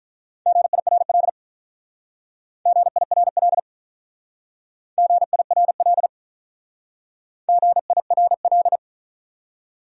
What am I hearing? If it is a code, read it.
Morse code, heard as a keyed tone, about 46 wpm, 700 Hz. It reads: GIRL GIRL GIRL GIRL